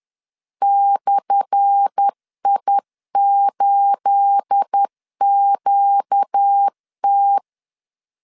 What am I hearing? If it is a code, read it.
Morse code, heard as a keyed tone, about 10 wpm, 800 Hz. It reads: /I8QT
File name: LEZIONI / 10 SIGNS / 10A SLASH 10wpm 2808